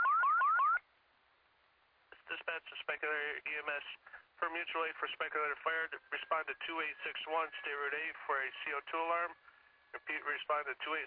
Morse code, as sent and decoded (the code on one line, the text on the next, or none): none